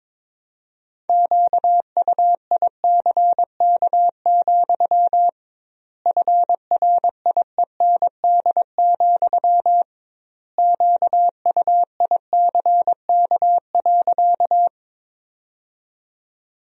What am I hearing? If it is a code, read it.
Morse code, heard as a keyed tone, about 22 wpm, 700 Hz. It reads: QUICK, FRIEND, QUICK.
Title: 2x Word-Spacing - Sentences from Top 500 Words - 22wpm